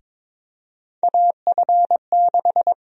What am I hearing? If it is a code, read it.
Morse code, heard as a keyed tone, about 22 wpm, 700 Hz. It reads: AF6
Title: USA Call Sign Prefixes - 22wpm